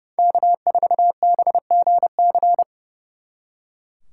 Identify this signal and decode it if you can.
Morse code, heard as a keyed tone, about 36 wpm, 700 Hz. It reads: K4BGC
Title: Speed Racing - USA 1x3 Call Signs 30wpm